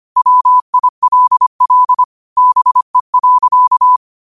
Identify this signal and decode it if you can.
Morse code, heard as a keyed tone, about 25 wpm, 1000 Hz. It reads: WILL BE.
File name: qotdfiles / 25wpm